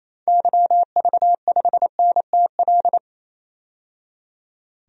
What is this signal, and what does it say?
Morse code, heard as a keyed tone, about 28 wpm, 700 Hz. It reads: YV5NTL
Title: DXCC-100 Call Signs - 6 Characters - 28wpm